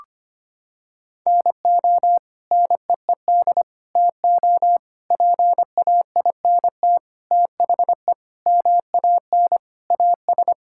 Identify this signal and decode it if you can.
Morse code, heard as a keyed tone, about 25 wpm, 700 Hz. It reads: NO NEED TO PAINT THE MAN AS
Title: Sentences from Top 1000 Words - Review Part 6 - 25wpm